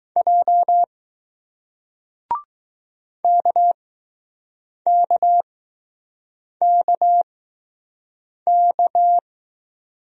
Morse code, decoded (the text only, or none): J KKKK